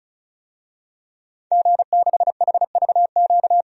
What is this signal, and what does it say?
Morse code, heard as a keyed tone, about 36 wpm, 700 Hz. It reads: G6HVQ
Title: DXCC-100 Call Signs - 5 Characters - 35wpm